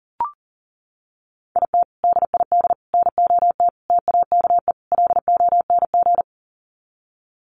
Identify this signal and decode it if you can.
Morse code, heard as a keyed tone, about 40 wpm, 700 Hz. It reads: IT DID NOT TAKE LONG
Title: Sets of 5 Words Encore2 40wpm